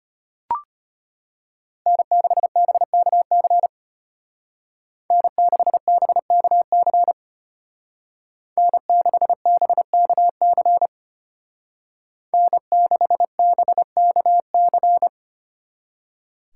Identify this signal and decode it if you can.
Morse code, heard as a keyed tone, about 33 wpm, 700 Hz. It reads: N6BKC N6BKC N6BKC N6BKC